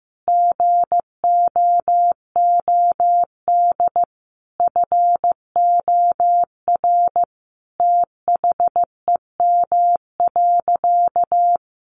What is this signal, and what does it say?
Morse code, heard as a keyed tone, about 15 wpm, 700 Hz. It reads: GOOD FOR THEM.
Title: Sentences from Top 1300 Words 15wpm